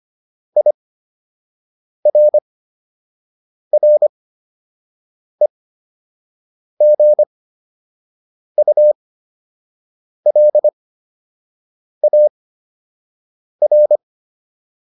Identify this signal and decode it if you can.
Morse code, heard as a keyed tone, about 25 wpm, spaced as for 6 wpm, 600 Hz. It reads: IRREGULAR